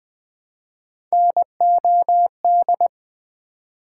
Morse code, heard as a keyed tone, about 20 wpm, 700 Hz. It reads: NOD
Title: Build A Word - 1 Letter At A Time - 3-Letter Words - 20wpm